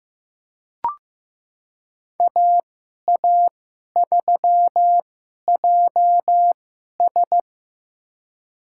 Morse code, heard as a keyed tone, about 15 wpm, spaced as for 11 wpm, 700 Hz. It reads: AA3JS
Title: USA Call Signs - 5 Characters - 2x Character Spacing - 15wpm